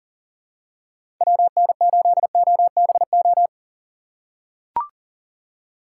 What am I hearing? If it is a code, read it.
Morse code, heard as a keyed tone, about 40 wpm, 690 Hz. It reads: WN8OBO
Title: USA Call Signs - 4x Character Spacing - 40wpm